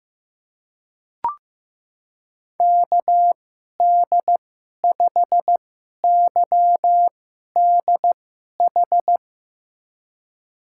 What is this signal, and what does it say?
Morse code, heard as a keyed tone, about 15 wpm, spaced as for 11 wpm, 700 Hz. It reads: KD5YDH
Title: USA Call Signs - 6 Characters - 2x Character Spacing - 15wpm